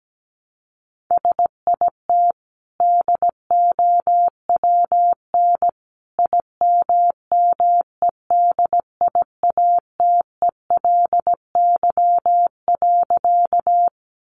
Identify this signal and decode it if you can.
Morse code, heard as a keyed tone, about 17 wpm, 700 Hz. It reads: SIT DOWN IMMEDIATELY.
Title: Sentences from Top 1100 Words 17wpm